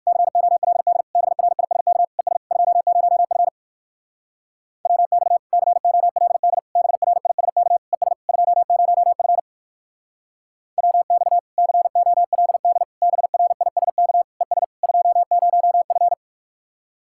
Morse code, heard as a keyed tone, about 64 wpm, 700 Hz. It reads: COLD BRISK ES 20F WX COLD BRISK ES 20F WX COLD BRISK ES 20F